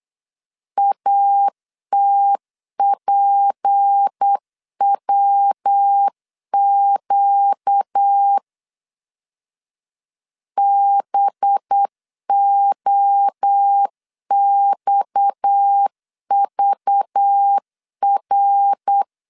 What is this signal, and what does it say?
Morse code, heard as a keyed tone, about 8 wpm, 800 Hz. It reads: ATPWQ BOXVR
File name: LEZIONI / 6 PKX0 / 6 PKX0 8wpm 3341